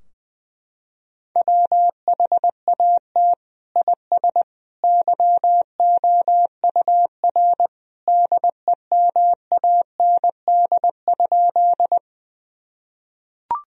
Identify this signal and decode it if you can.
Morse code, heard as a keyed tone, about 20 wpm, 700 Hz. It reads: WHAT IS YOUR DEMAND?
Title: Sentences from Top 1000 Words - Review Part 4 - 20wpm